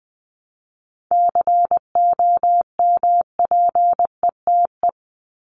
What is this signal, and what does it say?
Morse code, heard as a keyed tone, about 20 wpm, 700 Hz. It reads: COMPETE
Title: Top 1751-2000 Words 20wpm